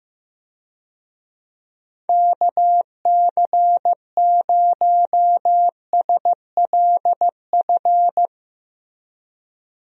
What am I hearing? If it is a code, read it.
Morse code, heard as a keyed tone, about 15 wpm, 700 Hz. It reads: KC0SLF
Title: USA Call Signs - 6 Characters - Repeated 3x - 15wpm